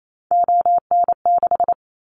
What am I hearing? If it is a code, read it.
Morse code, heard as a keyed tone, about 28 wpm, 700 Hz. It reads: ON6
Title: Callsign Prefixes for Top 100 DXCC - 28wpm